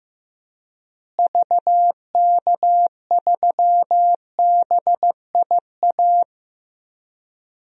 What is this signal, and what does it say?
Morse code, heard as a keyed tone, about 16 wpm, 700 Hz. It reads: VK3BIA
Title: DXCC-100 Call Signs - 6 Characters - Repeated 2x - 15wpm